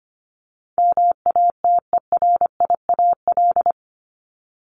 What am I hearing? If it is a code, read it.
Morse code, heard as a keyed tone, about 25 wpm, 700 Hz. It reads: MATERIAL
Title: ES:S Suffix - Top 100 Words with Base Words 25wpm